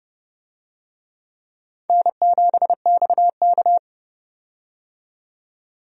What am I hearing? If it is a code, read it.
Morse code, heard as a keyed tone, about 30 wpm, 700 Hz. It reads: N7XK
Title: USA Call Signs - 4 Characters - 30wpm